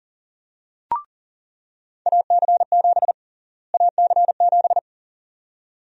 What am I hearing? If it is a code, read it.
Morse code, heard as a keyed tone, about 40 wpm, 700 Hz. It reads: AC7 AC7
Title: USA Call Sign Prefixes - Repeated 2x - 40wpm